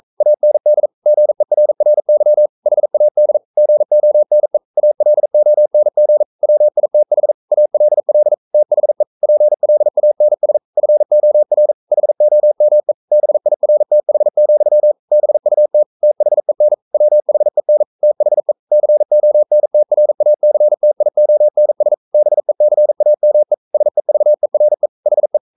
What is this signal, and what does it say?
Morse code, heard as a keyed tone, about 42 wpm, 600 Hz. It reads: AND GERRY HAD GONE ALONG WITH ALL THE PLANS FOR HOME BIRTH, BUT THEN WHEN THE CONTRACTIONS BECAME SEVERE HE
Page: Romanen Never af Ken Follet på engelsk.